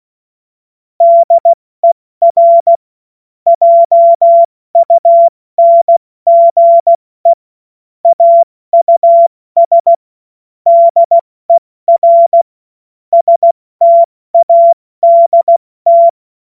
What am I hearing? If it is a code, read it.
Morse code, heard as a keyed tone, about 16 wpm, spaced as for 14 wpm, 680 Hz. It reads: DER JUNGE AUS DER STADT
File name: Morsekurs / Witze / Witz 07. März 2023 70 F80